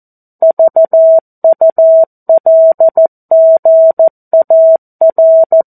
Morse code, heard as a keyed tone, about 14 wpm, 650 Hz. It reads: VULGAR
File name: 02-Chapter-02 14 WPM